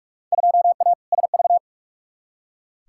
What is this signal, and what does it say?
Morse code, heard as a keyed tone, about 46 wpm, 700 Hz. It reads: 2A SV
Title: Field Day Exchanges - 45wpm